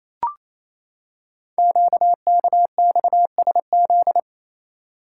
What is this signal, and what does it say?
Morse code, heard as a keyed tone, about 28 wpm, 700 Hz. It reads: QKXSZ
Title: Code Groups - 5 Letters - 28wpm